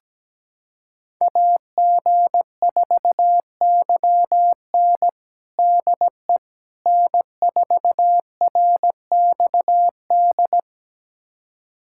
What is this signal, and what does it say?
Morse code, heard as a keyed tone, about 17 wpm, 700 Hz. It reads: AG4YN DE N4RXD
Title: USA Callsign DE Callsign - 5 Characters - 17wpm